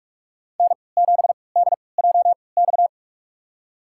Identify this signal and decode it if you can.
Morse code, heard as a keyed tone, about 44 wpm, spaced as for 25 wpm, 700 Hz. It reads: N7DJX
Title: USA Call Signs - 5 Characters - 3x Character Spacing - 45wpm